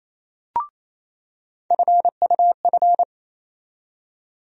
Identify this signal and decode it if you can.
Morse code, heard as a keyed tone, about 28 wpm, 700 Hz. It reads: FUF
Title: USA Call Sign Postfixes - 3 Letters 28wpm